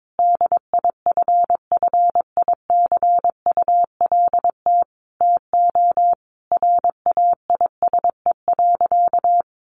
Morse code, heard as a keyed tone, about 22 wpm, 700 Hz. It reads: DIFFICULT TO RAISE.
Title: Sentences from Top 1400 Words 22wpm